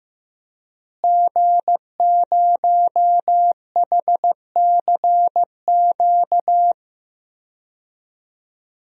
Morse code, heard as a keyed tone, about 15 wpm, 700 Hz. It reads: G0HCQ
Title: DXCC-100 Call Signs - 5 Characters - Repeated 3x - 15wpm